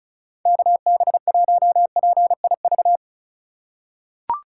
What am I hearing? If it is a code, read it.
Morse code, heard as a keyed tone, about 34 wpm, 700 Hz. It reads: KB1PIV
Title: USA Call Signs - 6 Characters - 2x Character Spacing - 35wpm